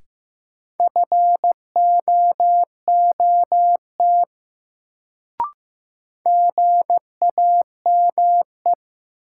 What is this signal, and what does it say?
Morse code, heard as a keyed tone, about 15 wpm, 700 Hz. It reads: FOOT GAME
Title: Sentences from Top 500 Words - New Word Drill - 15wpm